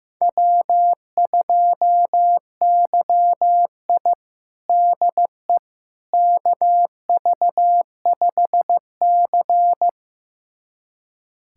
Morse code, heard as a keyed tone, about 15 wpm, 700 Hz. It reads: W2YI DE KV5C
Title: USA Callsign DE Callsign - 4 Characters - 15wpm